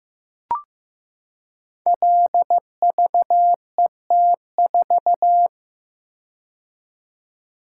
Morse code, heard as a keyed tone, about 15 wpm, 700 Hz. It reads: LVET4
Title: Code Groups - 5 Letter:Numbers - 15wpm